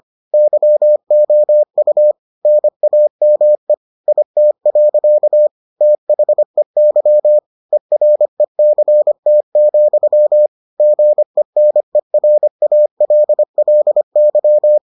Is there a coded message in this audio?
Yes: Morse code from 0.3 to 14.8 s, heard as a keyed tone, about 25 wpm, 600 Hz. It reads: YOU NAME IT. THEY ERECT, GENERALLY